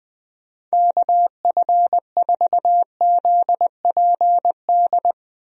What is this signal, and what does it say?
Morse code, heard as a keyed tone, about 21 wpm, 700 Hz. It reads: KF4ZPD
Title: USA Call Signs - 6 Characters - 20wpm